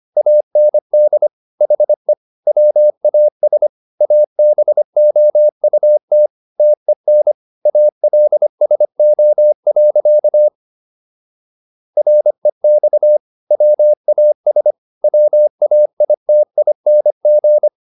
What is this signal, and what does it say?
Morse code, heard as a keyed tone, about 25 wpm, 600 Hz. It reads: AND HE WAS ABOUT TEN ALSO. REX WAS WAITING